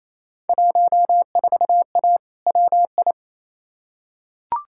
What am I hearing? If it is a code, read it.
Morse code, heard as a keyed tone, about 28 wpm, 700 Hz. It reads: 14A WI
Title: Field Day Exchanges Encore - 28wpm